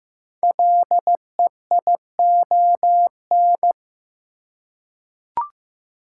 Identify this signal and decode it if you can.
Morse code, heard as a keyed tone, about 15 wpm, 700 Hz. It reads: LEION
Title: Code Groups - 5 Letter:Numbers - 15wpm